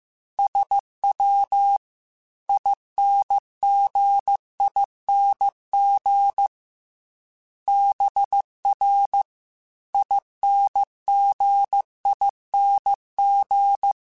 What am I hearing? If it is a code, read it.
Morse code, heard as a keyed tone, about 15 wpm, 800 Hz. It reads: SW INGING BR INGING